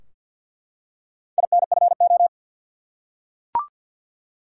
Morse code, heard as a keyed tone, about 51 wpm, 700 Hz. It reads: INFO
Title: Top 180 QSO Elements 50wpm